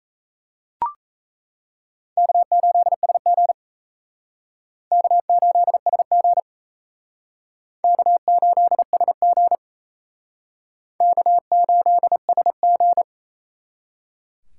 Morse code, heard as a keyed tone, about 37 wpm, 700 Hz. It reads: K8SG K8SG K8SG K8SG